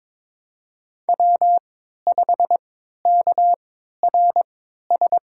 Morse code, heard as a keyed tone, about 22 wpm, spaced as for 12 wpm, 700 Hz. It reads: W5KRS
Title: USA Call Signs - 5 Characters - 3x Character Spacing - 22wpm